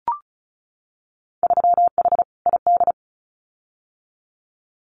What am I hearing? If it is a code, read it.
Morse code, heard as a keyed tone, about 36 wpm, 700 Hz. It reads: 3H ID
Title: Winter Field Day Exchanges Encore 35wpm